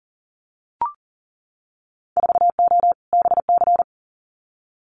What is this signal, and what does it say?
Morse code, heard as a keyed tone, about 41 wpm, 700 Hz. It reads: 4O BC
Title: Winter Field Day Exchanges Encore 40wpm